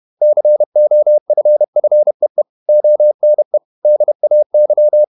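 Morse code, heard as a keyed tone, about 31 wpm, 600 Hz. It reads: COFFEE ONE DAY
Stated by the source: Morse code at 30 wpm, the novel Never by Ken Follett in English, from CW med Gnister.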